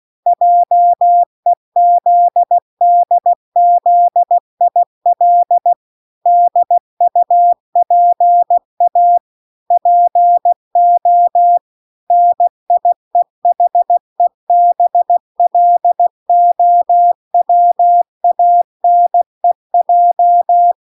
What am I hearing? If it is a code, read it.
Morse code, heard as a keyed tone, about 16 wpm, 700 Hz. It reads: JEZDZIL DUPA PO NIEHEBLOWANEJ